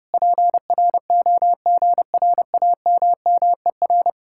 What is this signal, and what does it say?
Morse code, heard as a keyed tone, about 33 wpm, 700 Hz. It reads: PROGRAMMER